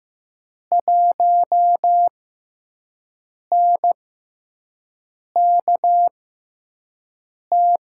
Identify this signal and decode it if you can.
Morse code, heard as a keyed tone, about 15 wpm, spaced as for 5 wpm, 700 Hz. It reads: 1NKT
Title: USA Call Signs - 5 Characters - 6x Character Spacing - 15wpm